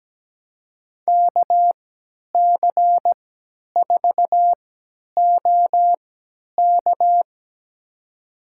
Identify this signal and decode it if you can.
Morse code, heard as a keyed tone, about 17 wpm, spaced as for 10 wpm, 700 Hz. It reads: KC4OK
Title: USA Call Signs - 5 Characters - 3x Character Spacing - 17wpm